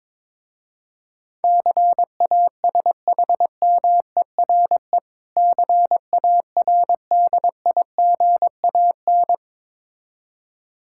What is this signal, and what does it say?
Morse code, heard as a keyed tone, about 22 wpm, 700 Hz. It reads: CASHMERE CARDIGAN
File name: Difficult Words with Hints - 22wpm